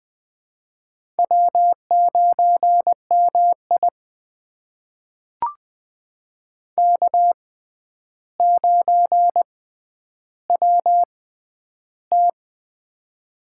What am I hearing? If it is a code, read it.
Morse code, heard as a keyed tone, about 21 wpm, spaced as for 10 wpm, 700 Hz. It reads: W9MI K9WT
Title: USA Call Signs - 4 Characters - 6x Character Spacing - 20wpm